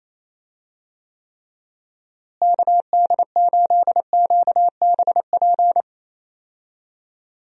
Morse code, heard as a keyed tone, about 28 wpm, 700 Hz. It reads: KD8QBP
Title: USA Call Signs - 6 Characters - Repeated 3x - 28wpm